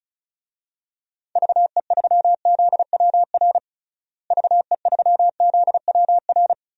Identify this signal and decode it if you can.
Morse code, heard as a keyed tone, about 36 wpm, 700 Hz. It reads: VE3ZWR VE3ZWR
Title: DXCC-100 Call Signs - 6 Characters - Repeated 3x - 35wpm